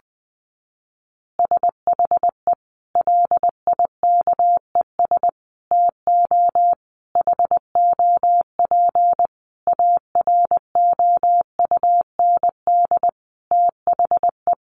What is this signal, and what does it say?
Morse code, heard as a keyed tone, about 20 wpm, 700 Hz. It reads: SHE LIKES TO HOP AROUND THE
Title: Easy Phrases 20wpm